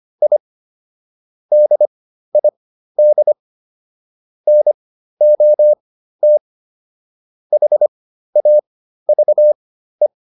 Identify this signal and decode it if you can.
Morse code, heard as a keyed tone, about 25 wpm, spaced as for 13 wpm, 600 Hz. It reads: I DID NOT HAVE